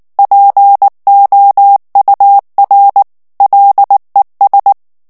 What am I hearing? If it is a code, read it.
Morse code, heard as a keyed tone, about 19 wpm, 800 Hz. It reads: POUR LES
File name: CW / FICHIERS-MP3 / FRANCAIS TEXTE CLAIR 2 VITESSE 18 MOTS MINUTES